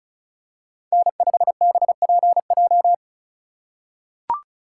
Morse code, heard as a keyed tone, about 36 wpm, 700 Hz. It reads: N5BPJ
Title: USA Call Signs - 5 Characters - 4x Character Spacing - 35wpm